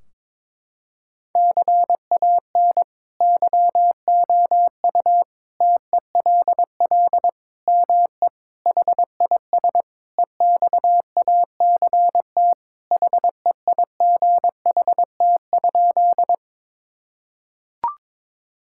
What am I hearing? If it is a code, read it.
Morse code, heard as a keyed tone, about 21 wpm, 700 Hz. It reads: CAN YOU TELL ME HIS EXACT HEIGHT?
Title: Sentences from Top 1600 Words 22wpm